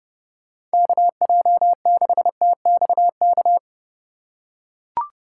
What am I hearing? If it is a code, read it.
Morse code, heard as a keyed tone, about 30 wpm, 700 Hz. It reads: KJ6TXK
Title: USA Call Signs - 6 Characters - 2x Character Spacing - 30wpm